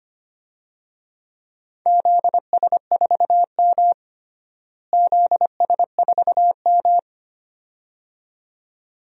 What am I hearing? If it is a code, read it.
Morse code, heard as a keyed tone, about 25 wpm, 700 Hz. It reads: ZS4M ZS4M
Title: DXCC-100 Call Signs - 4 Characters - Repeated 2x - 25wpm